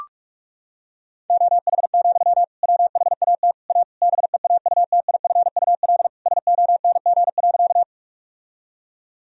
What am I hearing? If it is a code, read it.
Morse code, heard as a keyed tone, about 41 wpm, 700 Hz. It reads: OH, WHAT A BEAUTIFUL SONG.
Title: Sentences from Top 1000 Words - Review Part 2 - 45wpm